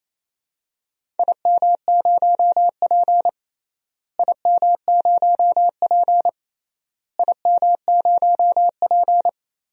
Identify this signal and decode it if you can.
Morse code, heard as a keyed tone, about 28 wpm, 700 Hz. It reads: IM0P IM0P IM0P
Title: DXCC-100 Call Signs - 4 Characters - Repeated 3x - 28wpm